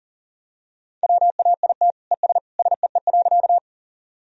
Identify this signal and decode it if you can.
Morse code, heard as a keyed tone, about 40 wpm, 700 Hz. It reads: WAIT ES SEE.
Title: Sentences from Top 500 Words and Abbreviations 40wpm